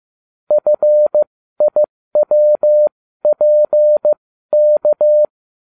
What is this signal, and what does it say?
Morse code, heard as a keyed tone, about 15 wpm, spaced as for 13 wpm, 600 Hz. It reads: FIWPK